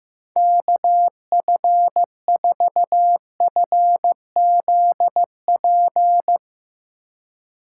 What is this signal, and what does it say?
Morse code, heard as a keyed tone, about 15 wpm, 700 Hz. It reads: KF4FZP
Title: Speed Racing - USA 2x3 Call Signs 15wpm